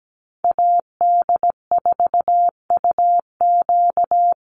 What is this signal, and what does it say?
Morse code, heard as a keyed tone, about 17 wpm, 700 Hz. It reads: AD4UQ
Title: USA Call Signs Repeated 3x 17wpm